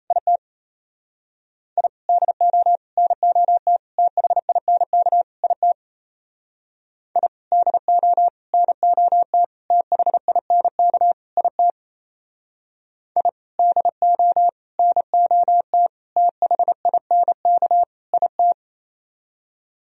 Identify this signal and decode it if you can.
Morse code, heard as a keyed tone, about 32 wpm, 700 Hz. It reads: IT I DO NOT THINK IT I DO NOT THINK IT I DO NOT THINK IT
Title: Speed-Racing - Sets of 5 Words 28wpm